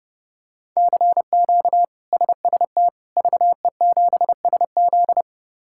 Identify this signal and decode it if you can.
Morse code, heard as a keyed tone, about 30 wpm, 700 Hz. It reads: CQ SST VE7SZ
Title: K1USN SST QSO Practice - 30wpm